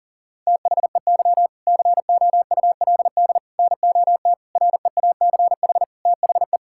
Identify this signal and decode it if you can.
Morse code, heard as a keyed tone, about 37 wpm, 700 Hz. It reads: THEY COULD NOT REACH THE
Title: Sentences from Top 1000 Words - Review Part 6 - 40wpm